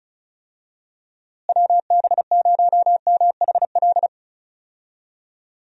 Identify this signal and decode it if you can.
Morse code, heard as a keyed tone, about 34 wpm, 700 Hz. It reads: WB0MHL
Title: USA Call Signs - 6 Characters - Repeated 2x - 35wpm